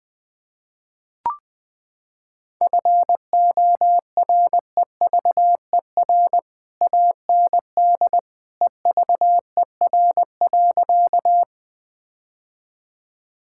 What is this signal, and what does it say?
Morse code, heard as a keyed tone, about 20 wpm, 700 Hz. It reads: FOREVER AND EVER.
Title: Sentences from Top 1600 Words 20wpm